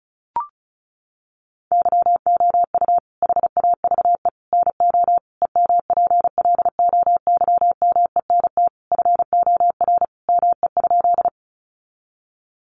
Morse code, heard as a keyed tone, about 34 wpm, 700 Hz. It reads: YOU HAVE NO EMPLOYMENT FOR ME?
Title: Sentences from Top 1500 Words 35wpm